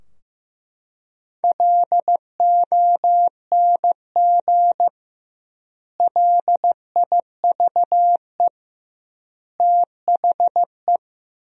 Morse code, heard as a keyed tone, about 15 wpm, 700 Hz. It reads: LONG LIVE THE
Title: 2x Word-Spacing - Sentences from Top 400 Words - 15wpm